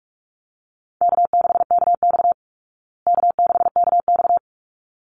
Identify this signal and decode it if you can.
Morse code, heard as a keyed tone, about 44 wpm, 700 Hz. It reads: K6KX K6KX
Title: Easy Call Signs Repeated 2x - 45wpm